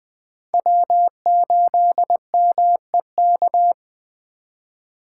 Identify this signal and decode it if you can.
Morse code, heard as a keyed tone, about 20 wpm, 700 Hz. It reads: W8MEK
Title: Speed Racing - USA 1x3 Call Signs 17wpm